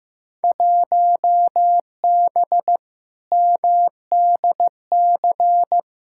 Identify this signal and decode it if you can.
Morse code, heard as a keyed tone, about 15 wpm, 700 Hz. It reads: 1B MDC
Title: Field Day Exchanges Encore - 15wpm